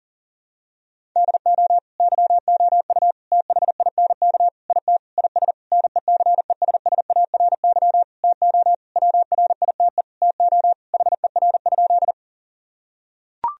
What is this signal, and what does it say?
Morse code, heard as a keyed tone, about 40 wpm, 700 Hz. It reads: DO YOU THINK IT IS NECESSARY TO WRITE TO HER?
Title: Sentences from Top 1000 Words - Review Part 1 - 40wpm